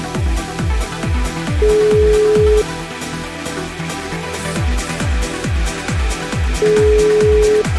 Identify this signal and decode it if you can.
Ringback tone (Europe, 425 Hz), with noise.